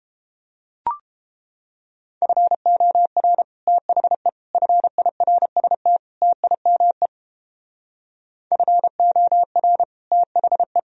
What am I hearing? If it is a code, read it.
Morse code, heard as a keyed tone, about 29 wpm, 700 Hz. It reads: FOR THE FIRST TIME FOR THE